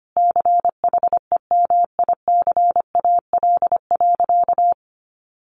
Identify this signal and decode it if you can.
Morse code, heard as a keyed tone, about 25 wpm, 700 Hz. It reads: CHEMICAL.
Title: Sentences from Top 1200 Words 25wpm